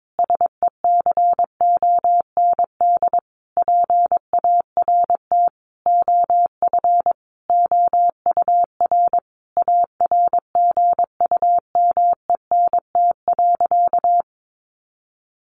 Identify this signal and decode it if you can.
Morse code, heard as a keyed tone, about 22 wpm, 700 Hz. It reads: SECOND PART OF OUR ARGUMENT.
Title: Sentences from Top 1400 Words 22wpm